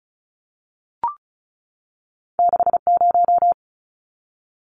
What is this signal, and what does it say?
Morse code, heard as a keyed tone, about 34 wpm, 700 Hz. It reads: 60
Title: Numbers - 2-Digit 35wpm